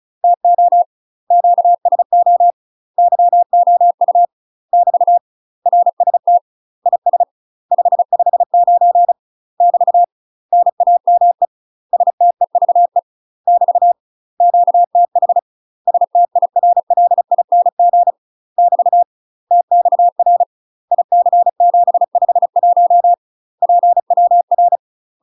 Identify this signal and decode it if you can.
Morse code, heard as a keyed tone, about 34 wpm, 700 Hz. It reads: TO QSO YOU <BT> RST IS 559 <BT> NAME STEVE <BT> QTH STIRLING <BT> TXR IC751 PWR
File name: UK MASTER QSOs TRAINING / 35WPM / UK MASTER QSOs 0039